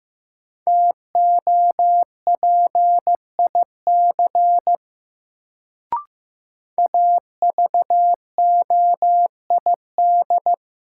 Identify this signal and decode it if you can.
Morse code, heard as a keyed tone, about 15 wpm, 700 Hz. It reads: TOPIC AVOID
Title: Sentences from Top 1200 Words - New Word Drill Alt - 15wpm